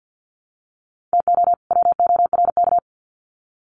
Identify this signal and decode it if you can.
Morse code, heard as a keyed tone, about 50 wpm, 700 Hz. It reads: TO WORK
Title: Sets of 2 Words 50wpm